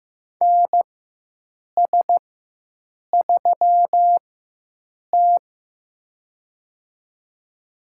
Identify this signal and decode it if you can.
Morse code, heard as a keyed tone, about 15 wpm, spaced as for 7 wpm, 700 Hz. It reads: NS3T
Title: USA Call Signs - 4 Characters - 4x Character Spacing - 15wpm